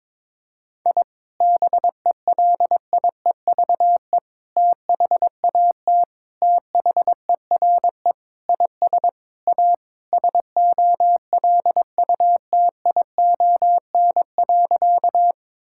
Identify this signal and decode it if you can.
Morse code, heard as a keyed tone, about 22 wpm, 700 Hz. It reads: I BELIEVE THAT THERE IS A SOLUTION.
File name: practice / 22wpm / Sentences with 4-Word Phrases - 22wpm